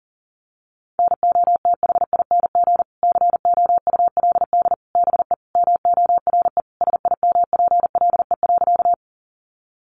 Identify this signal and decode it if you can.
Morse code, heard as a keyed tone, about 40 wpm, 700 Hz. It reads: NOTHING COULD BE MORE SIMPLE.
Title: Sentences from Top 500 Words 40wpm